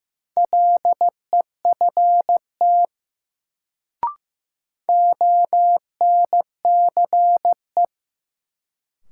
Morse code, heard as a keyed tone, about 15 wpm, 700 Hz. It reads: LEFT ONCE
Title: Sentences from Top 300 Words - New Word Drill - 15wpm